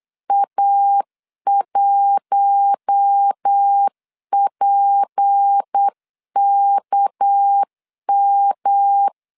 Morse code, heard as a keyed tone, about 8 wpm, 800 Hz. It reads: A1PKM
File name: LEZIONI / 7 GF9Z / 7 G 8wpm 3118